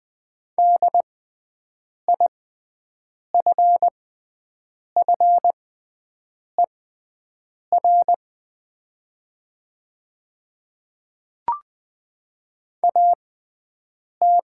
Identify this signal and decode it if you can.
Morse code, heard as a keyed tone, about 20 wpm, spaced as for 7 wpm, 700 Hz. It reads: DIFFER AT
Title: 6x Character Spacing - Top 200 Words - 20wpm